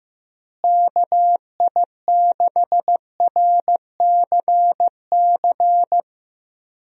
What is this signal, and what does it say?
Morse code, heard as a keyed tone, about 16 wpm, 700 Hz. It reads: KI6RCC
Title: USA Call Signs - 6 Characters - Repeated 2x - 15wpm